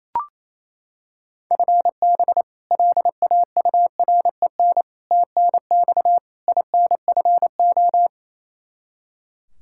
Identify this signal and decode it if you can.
Morse code, heard as a keyed tone, about 28 wpm, 700 Hz. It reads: FB LAUREN TNX INFO
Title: K1USN SST QSO Practice 28wpm